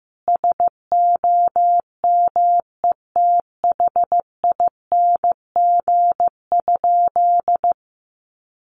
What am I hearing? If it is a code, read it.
Morse code, heard as a keyed tone, about 15 wpm, 700 Hz. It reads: SOMETHING?
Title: Sentences from Top 1100 Words 15wpm